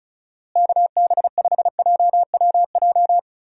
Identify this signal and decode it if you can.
Morse code, heard as a keyed tone, about 36 wpm, 700 Hz. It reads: KB5JWJ
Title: USA Call Signs - 6 Characters - Repeated 2x - 35wpm